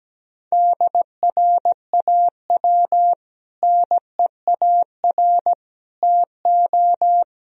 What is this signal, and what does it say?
Morse code, heard as a keyed tone, about 17 wpm, 700 Hz. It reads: DRAW NEAR TO
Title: Sentences from Top 1000 Words - Review Part 1 - 17wpm